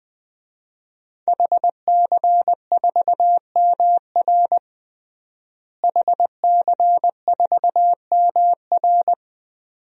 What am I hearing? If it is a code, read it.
Morse code, heard as a keyed tone, about 20 wpm, 700 Hz. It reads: HC4MR HC4MR
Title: DXCC-100 Call Signs - 5 Characters - Repeated 2x - 20wpm